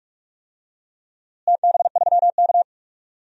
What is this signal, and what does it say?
Morse code, heard as a keyed tone, about 46 wpm, 700 Hz. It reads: TB3K
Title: DXCC-100 Call Signs - 4 Characters - Repeated 2x - 45wpm